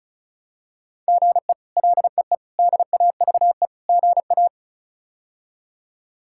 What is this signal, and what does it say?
Morse code, heard as a keyed tone, about 34 wpm, 700 Hz. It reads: GE LEE DAVE GA
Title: K1USN SST QSO Practice - 35wpm